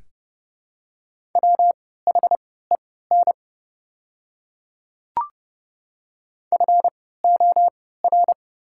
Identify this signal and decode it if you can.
Morse code, heard as a keyed tone, about 30 wpm, spaced as for 17 wpm, 690 Hz. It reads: WHEN FOR
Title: 3x Character Spacing - Top 100 Words - 30wpm